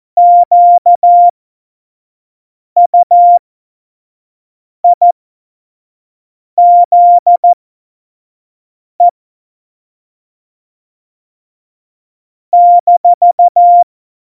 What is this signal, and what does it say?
Morse code, heard as a keyed tone, about 14 wpm, spaced as for 5 wpm, 700 Hz. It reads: QUIZE -
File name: Morsekurs / Witze / Witz 07 Juli 2020, 25 F70